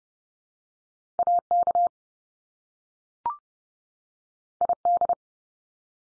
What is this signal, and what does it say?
Morse code, heard as a keyed tone, about 30 wpm, 700 Hz. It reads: AK ID
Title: US States 30WPM